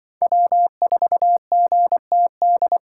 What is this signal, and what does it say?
Morse code, heard as a keyed tone, about 25 wpm, 700 Hz. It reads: W4GTD
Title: Speed Racing - USA 1x3 Call Signs 20wpm